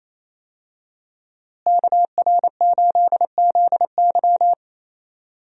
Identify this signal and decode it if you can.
Morse code, heard as a keyed tone, about 27 wpm, 700 Hz. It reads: KR8ZY
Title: USA Call Signs - 5 Characters - Repeated 3x - 28wpm